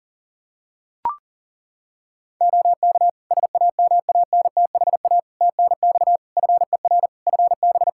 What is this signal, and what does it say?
Morse code, heard as a keyed tone, about 40 wpm, 700 Hz. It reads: OK SAMANTHA TNX FER FB
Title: Practice QSO - Exchange 4 - Ending - 40wpm